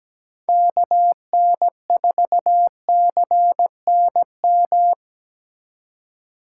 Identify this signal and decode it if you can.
Morse code, heard as a keyed tone, about 17 wpm, 700 Hz. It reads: KN4CNM
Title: USA Call Signs - 2x Character Spacing - 17wpm